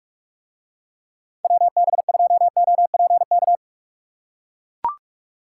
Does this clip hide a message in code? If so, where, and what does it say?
Yes: Morse code from 1.4 to 3.6 s, heard as a keyed tone, about 44 wpm, 700 Hz. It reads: WB2OPK